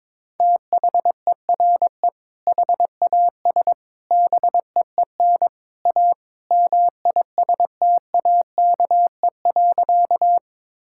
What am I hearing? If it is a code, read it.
Morse code, heard as a keyed tone, about 21 wpm, 700 Hz. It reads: THERE HAS BEEN A MISTAKE.
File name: practice / 22wpm / Sentences with 3-Word Phrases - 22wpm